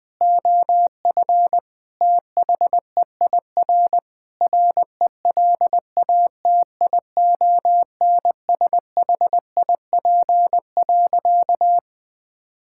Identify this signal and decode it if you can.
Morse code, heard as a keyed tone, about 20 wpm, 700 Hz. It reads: OF THEIR RELATIONSHIP.
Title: Sentences with 5-Word Phrases - 20wpm